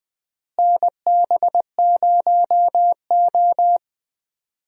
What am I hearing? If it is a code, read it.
Morse code, heard as a keyed tone, about 20 wpm, 700 Hz. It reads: NB0O
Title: USA Call Signs - 4 Characters - Repeated 2x - 20wpm